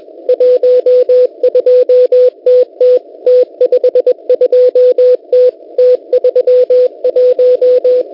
Morse code, heard as a keyed tone, about 21 wpm, 500 Hz. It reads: 12TT T52T T31
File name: Unid- -0701utc-4735khz